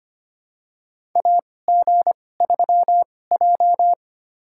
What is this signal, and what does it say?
Morse code, heard as a keyed tone, about 25 wpm, spaced as for 18 wpm, 700 Hz. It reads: AG3J
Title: USA Call Signs - 4 Characters - 2x Character Spacing - 25wpm